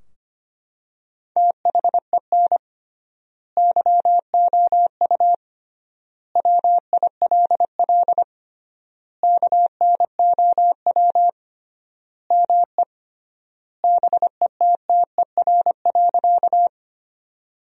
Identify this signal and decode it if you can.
Morse code, heard as a keyed tone, about 25 wpm, 700 Hz. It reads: THEN YOU WILL KNOW ME BETTER.
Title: 3x Word-Spacing - Sentences from Top 400 Words - 25wpm